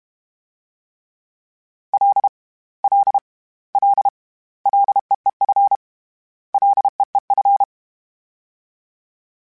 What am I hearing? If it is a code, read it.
Morse code, heard as a keyed tone, about 32 wpm, 800 Hz. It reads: L L L LEEF LEEF
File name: Short Runs / Short Runs-mp3s / SinglePrefixes @30